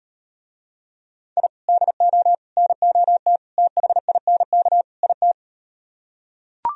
Morse code, heard as a keyed tone, about 38 wpm, 700 Hz. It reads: I DO NOT THINK IT